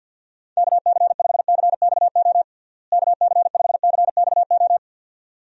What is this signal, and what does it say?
Morse code, heard as a keyed tone, about 49 wpm, 700 Hz. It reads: KC5CXO KC5CXO
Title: USA Call Signs - 6 Characters - Repeated 2x - 50wpm